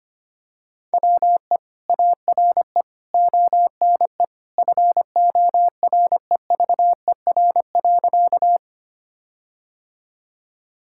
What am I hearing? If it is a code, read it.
Morse code, heard as a keyed tone, about 25 wpm, 700 Hz. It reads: WE ARE ONE FOREVER.
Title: Sentences from Top 1600 Words 25wpm